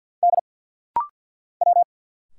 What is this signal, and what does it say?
Morse code, heard as a keyed tone, about 50 wpm, 690 Hz. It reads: D W
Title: Single Letters - Rapid-Fire 50wpm